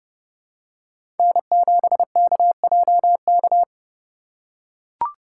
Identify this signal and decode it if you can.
Morse code, heard as a keyed tone, about 32 wpm, 700 Hz. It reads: N7KJK
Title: USA Call Signs - 5 Characters - 3x Character Spacing - 30wpm